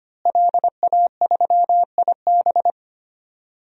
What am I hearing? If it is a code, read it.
Morse code, heard as a keyed tone, about 25 wpm, 700 Hz. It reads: LA3IB
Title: DXCC-100 Call Signs - 5 Characters - Repeated 3x - 25wpm